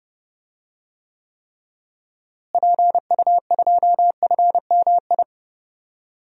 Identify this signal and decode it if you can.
Morse code, heard as a keyed tone, about 30 wpm, 700 Hz. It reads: PU2FMI
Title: DXCC-100 Call Signs - 6 Characters - Repeated 3x - 30wpm